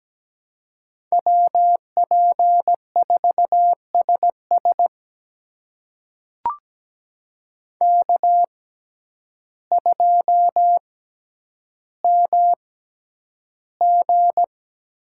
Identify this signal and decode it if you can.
Morse code, heard as a keyed tone, about 17 wpm, spaced as for 9 wpm, 700 Hz. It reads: WP4SS K2MG
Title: USA Call Signs - 5 Characters - 6x Character Spacing - 17wpm